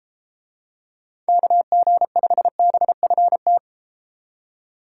Morse code, heard as a keyed tone, about 33 wpm, 700 Hz. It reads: KG5BFT